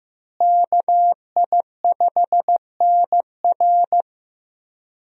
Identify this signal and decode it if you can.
Morse code, heard as a keyed tone, about 16 wpm, 700 Hz. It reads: KI5NR
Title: USA Call Signs - 5 Characters - Repeated 2x - 15wpm